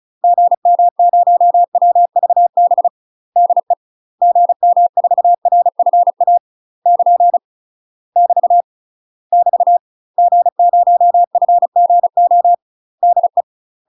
Morse code, heard as a keyed tone, about 34 wpm, 700 Hz. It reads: GM0WVB DE GM4RFA <KN> <BT> <BT> G0FGO DE
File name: UK MASTER QSOs TRAINING / 35WPM / UK MASTER QSOs 0035